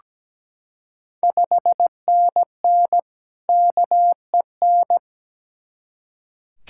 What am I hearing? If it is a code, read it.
Morse code, heard as a keyed tone, about 17 wpm, 700 Hz. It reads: 5NN KEN
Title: Ontario QSO Party Exchange 17wpm